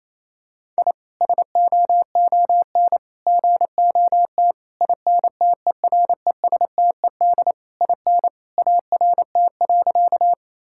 Morse code, heard as a keyed tone, about 28 wpm, 700 Hz. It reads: I SOON GOT INTERESTED IN ART.
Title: Sentences from Top 1000 Words - Review Part 6 - 28wpm